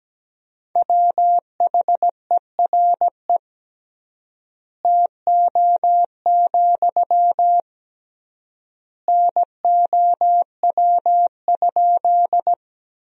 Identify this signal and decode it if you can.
Morse code, heard as a keyed tone, about 17 wpm, 700 Hz. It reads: WHERE TO, NOW?
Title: 3x Word-Spacing - Sentences from Top 200 Words - 17wpm